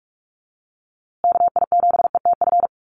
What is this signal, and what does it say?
Morse code, heard as a keyed tone, about 44 wpm, 700 Hz. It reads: KI7ETF
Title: USA Call Signs Repeated 3x 45wpm